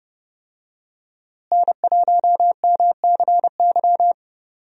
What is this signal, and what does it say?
Morse code, heard as a keyed tone, about 30 wpm, 700 Hz. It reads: N1MCY
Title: USA Call Signs - 5 Characters - Repeated 3x - 30wpm